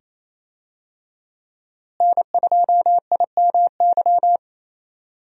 Morse code, heard as a keyed tone, about 28 wpm, 700 Hz. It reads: N2IMY
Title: USA Call Signs - 5 Characters - Repeated 2x - 28wpm